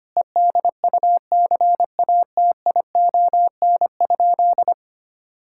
Morse code, heard as a keyed tone, about 25 wpm, 700 Hz. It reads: EDUCATION?
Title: Sentences from Top 1000 Words - Review Part 6 - 25wpm